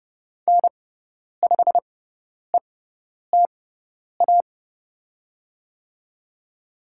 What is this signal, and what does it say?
Morse code, heard as a keyed tone, about 29 wpm, spaced as for 10 wpm, 690 Hz. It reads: N5ETA
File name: Lesson 17g - Letter u - Call Signs 30wpm